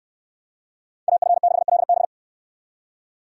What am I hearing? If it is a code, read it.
Morse code, heard as a keyed tone, about 68 wpm, 700 Hz. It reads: AF6LB